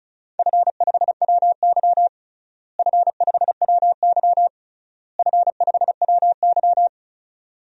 Morse code, heard as a keyed tone, about 36 wpm, 700 Hz. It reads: F5WY F5WY F5WY
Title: DXCC-100 Call Signs - 4 Characters - Repeated 3x - 35wpm